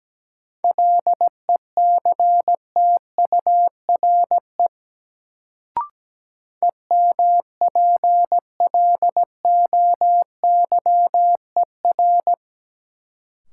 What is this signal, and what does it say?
Morse code, heard as a keyed tone, about 17 wpm, 700 Hz. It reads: LECTURE EMPLOYER
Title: Sentences from Top 1600 Words - New Word Drill - 17wpm